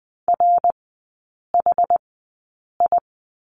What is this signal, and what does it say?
Morse code, heard as a keyed tone, about 20 wpm, spaced as for 8 wpm, 700 Hz. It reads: RHI
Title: 133 Lesson 22e - Question Mark - Three Characters 20wpm